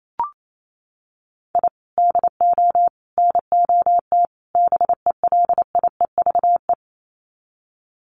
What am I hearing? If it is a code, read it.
Morse code, heard as a keyed tone, about 28 wpm, 700 Hz. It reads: I DO NOT BELIEVE
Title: Sets of 4 Words 28wpm